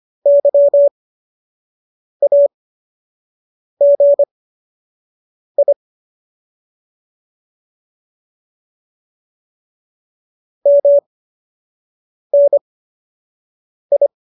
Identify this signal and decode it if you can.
Morse code, heard as a keyed tone, about 25 wpm, spaced as for 6 wpm, 570 Hz. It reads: YAGI MNI
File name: qso012 06